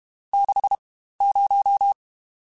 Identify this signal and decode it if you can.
Morse code, heard as a keyed tone, about 32 wpm, spaced as for 15 wpm, 800 Hz. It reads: 60
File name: Spaced Characters / Spaced Characters-mp3s / Weather5 22 sp @30